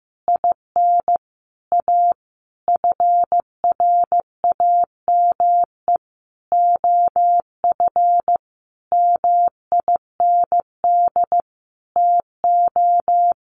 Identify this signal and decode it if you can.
Morse code, heard as a keyed tone, about 15 wpm, 700 Hz. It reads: IN A FRAME OF MIND TO
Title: Sentences from Top 1400 Words 15wpm